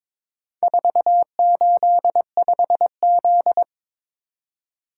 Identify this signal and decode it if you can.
Morse code, heard as a keyed tone, about 22 wpm, 700 Hz. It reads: 485Z
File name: practice / 22wpm / Four Letter-Number Combinations - 22wpm